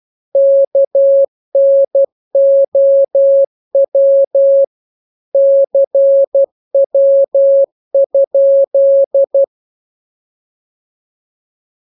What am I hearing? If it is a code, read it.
Morse code, heard as a keyed tone, about 12 wpm, 550 Hz. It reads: KNOW CW?